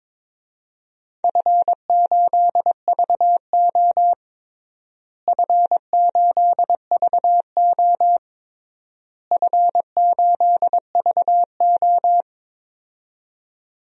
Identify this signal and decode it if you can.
Morse code, heard as a keyed tone, about 22 wpm, 700 Hz. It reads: F8VO F8VO F8VO
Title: DXCC-100 Call Signs - 4 Characters - Repeated 3x - 22wpm